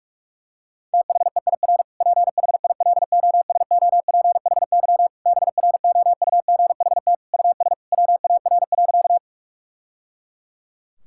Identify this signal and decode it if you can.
Morse code, heard as a keyed tone, about 46 wpm, 700 Hz. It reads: THEIR PHILOSOPHY BROUGHT US WAR.
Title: Sentences from Top 1000 Words - Review Part 6 - 45wpm